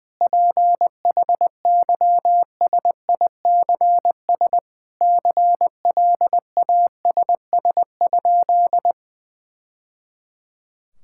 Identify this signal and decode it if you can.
Morse code, heard as a keyed tone, about 19 wpm, 700 Hz. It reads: PHYSICS CLASS?
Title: Sentences from Top 1000 Words - Review Part 5 - 20wpm